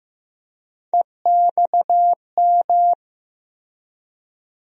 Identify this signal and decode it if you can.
Morse code, heard as a keyed tone, about 15 wpm, 700 Hz. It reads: EXM
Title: USA Call Sign Postfixes - 3 Letters 15wpm